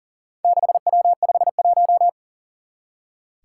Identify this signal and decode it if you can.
Morse code, heard as a keyed tone, about 39 wpm, 700 Hz. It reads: 6W51
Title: Four Challenging Letter-Number Combinations 40wpm